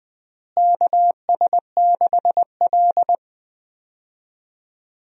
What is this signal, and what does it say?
Morse code, heard as a keyed tone, about 20 wpm, 700 Hz. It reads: KS6L
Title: USA Call Signs - 4 Characters - Repeated 2x - 20wpm